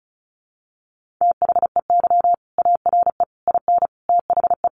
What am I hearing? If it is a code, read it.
Morse code, heard as a keyed tone, about 34 wpm, 700 Hz. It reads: THEY ARE IN THE
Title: Sentences from Top 600 Words 35wpm